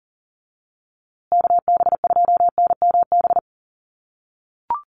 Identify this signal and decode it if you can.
Morse code, heard as a keyed tone, about 40 wpm, 700 Hz. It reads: KB2NMB
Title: USA Call Signs Repeated 2x 40wpm